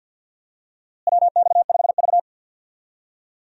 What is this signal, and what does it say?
Morse code, heard as a keyed tone, about 52 wpm, 700 Hz. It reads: WX5V
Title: USA Call Signs - 4 Characters - Repeated 2x - 50wpm